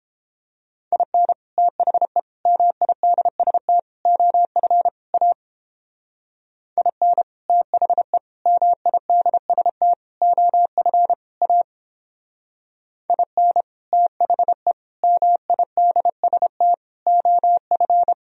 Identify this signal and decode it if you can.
Morse code, heard as a keyed tone, about 31 wpm, 700 Hz. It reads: IN THE MIDST OF A IN THE MIDST OF A IN THE MIDST OF